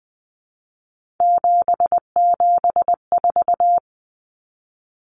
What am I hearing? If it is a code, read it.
Morse code, heard as a keyed tone, about 20 wpm, 700 Hz. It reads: 774
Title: Common Cut Numbers - 3-Digit 20wpm